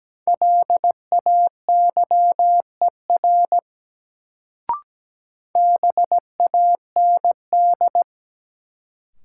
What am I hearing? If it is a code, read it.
Morse code, heard as a keyed tone, about 17 wpm, 700 Hz. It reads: LAYER BAND
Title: Sentences from Top 1600 Words - New Word Drill - 17wpm